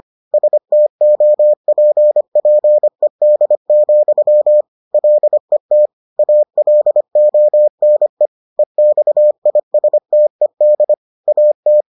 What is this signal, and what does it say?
Morse code, heard as a keyed tone, about 25 wpm, 600 Hz. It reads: STOPPED, LET ALONE EXISTED AT